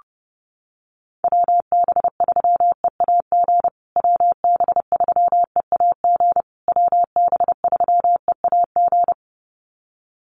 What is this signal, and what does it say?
Morse code, heard as a keyed tone, about 30 wpm, 700 Hz. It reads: WB3EAG WB3EAG WB3EAG
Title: USA Call Signs Repeated 3x 30wpm